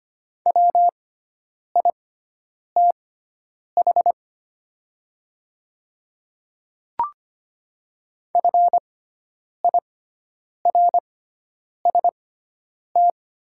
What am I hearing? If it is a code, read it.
Morse code, heard as a keyed tone, about 25 wpm, spaced as for 9 wpm, 700 Hz. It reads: WITH FIRST
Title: 6x Character Spacing - Top 100 Words - 25wpm